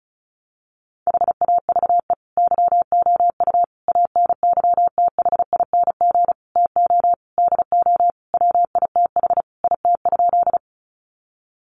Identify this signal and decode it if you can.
Morse code, heard as a keyed tone, about 34 wpm, 700 Hz. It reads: HAVE YOU ANYTHING TO DO WITH IT?
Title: Sentences from Top 1000 Words 35wpm